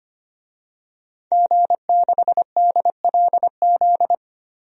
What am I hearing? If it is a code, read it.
Morse code, heard as a keyed tone, about 25 wpm, 700 Hz. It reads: G6DLZ
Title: DXCC-100 Call Signs - 5 Characters - Repeated 3x - 25wpm